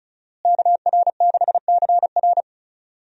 Morse code, heard as a keyed tone, about 36 wpm, 700 Hz. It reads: KR6CR
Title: USA Call Signs - 5 Characters - Repeated 3x - 35wpm